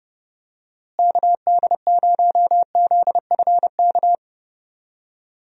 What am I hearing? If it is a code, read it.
Morse code, heard as a keyed tone, about 31 wpm, 700 Hz. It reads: KD0ZFK
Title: USA Call Signs - 6 Characters - Repeated 3x - 30wpm